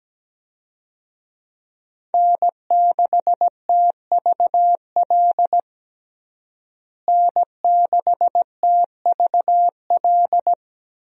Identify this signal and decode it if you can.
Morse code, heard as a keyed tone, about 17 wpm, 700 Hz. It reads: N6TVL N6TVL
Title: USA Call Signs - 5 Characters - Repeated 2x - 17wpm